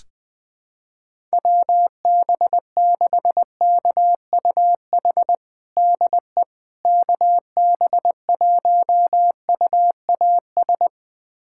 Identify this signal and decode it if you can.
Morse code, heard as a keyed tone, about 20 wpm, 700 Hz. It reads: WB6KUH DE KB1UAS
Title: USA Callsign DE Callsign - 6 Characters - 20wpm